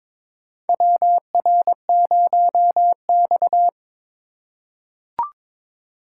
Morse code, heard as a keyed tone, about 22 wpm, 700 Hz. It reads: WR0X
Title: USA Call Signs - 4x Character Spacing - 22wpm